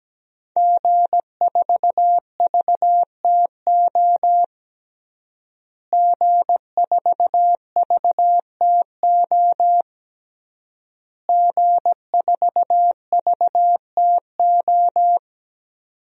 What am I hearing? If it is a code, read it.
Morse code, heard as a keyed tone, about 17 wpm, 700 Hz. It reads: G4VTO G4VTO G4VTO
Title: DXCC-100 Call Signs - 5 Characters - Repeated 3x - 17wpm